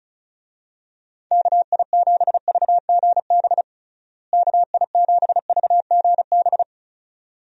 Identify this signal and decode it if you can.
Morse code, heard as a keyed tone, about 36 wpm, 700 Hz. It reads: KI7VGB KI7VGB
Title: USA Call Signs - 6 Characters - Repeated 2x - 35wpm